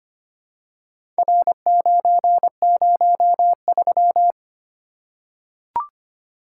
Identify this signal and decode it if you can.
Morse code, heard as a keyed tone, about 25 wpm, 700 Hz. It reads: R903
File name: practice / 25wpm / Four Letter-Number Combinations - 25wpm